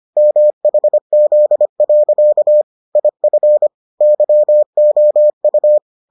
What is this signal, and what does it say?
Morse code, heard as a keyed tone, about 25 wpm, 600 Hz. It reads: MHZ. IF YOU